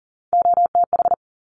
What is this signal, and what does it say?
Morse code, heard as a keyed tone, about 40 wpm, 700 Hz. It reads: OTH
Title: Top Three Letter Combinations in Words 40wpm